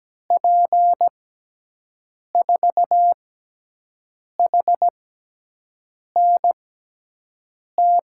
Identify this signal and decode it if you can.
Morse code, heard as a keyed tone, about 17 wpm, spaced as for 6 wpm, 700 Hz. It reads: P4HNT
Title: USA Call Signs - 6 Characters - 6x Character Spacing - 17wpm